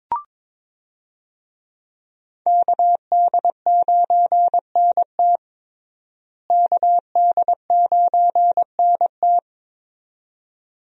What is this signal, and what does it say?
Morse code, heard as a keyed tone, about 22 wpm, 700 Hz. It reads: KD9NT KD9NT
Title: USA Call Signs - 5 Characters - Repeated 2x - 22wpm